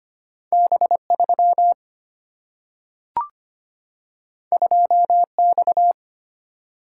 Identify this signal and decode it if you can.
Morse code, heard as a keyed tone, about 25 wpm, 690 Hz. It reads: B3 2X
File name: Two Letter-Number Combinations - 25wpm